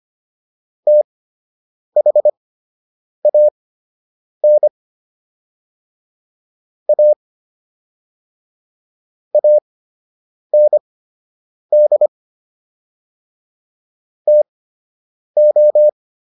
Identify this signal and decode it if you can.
Morse code, heard as a keyed tone, about 25 wpm, spaced as for 8 wpm, 600 Hz. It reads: THAN A AND TO
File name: WD104 08